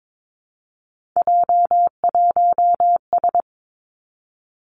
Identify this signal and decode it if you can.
Morse code, heard as a keyed tone, about 22 wpm, 700 Hz. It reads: J1S
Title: Three Challenging Letter-Number Combinations 22wpm